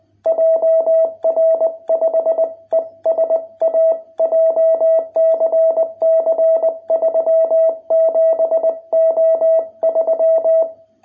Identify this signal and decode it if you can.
Morse code, heard as a keyed tone, about 20 wpm, 660 Hz. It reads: JR5ESA JCC 37O3